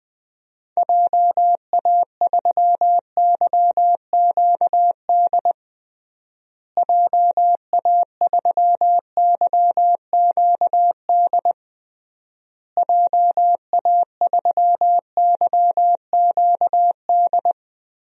Morse code, heard as a keyed tone, about 21 wpm, 700 Hz. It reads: JA3YQD JA3YQD JA3YQD
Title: DXCC-100 Call Signs - 6 Characters - Repeated 3x - 20wpm